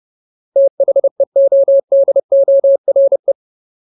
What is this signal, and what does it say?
Morse code, heard as a keyed tone, about 30 wpm, 550 Hz. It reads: THEODORE
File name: Ultimate Word List / 30WPM Ultimate